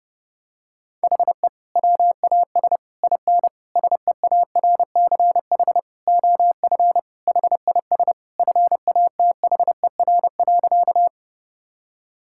Morse code, heard as a keyed tone, about 30 wpm, 700 Hz. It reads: HE WAS IN SEARCH OF HIS FATHER.
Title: Sentences from Top 1000 Words - Review Part 5 - 30wpm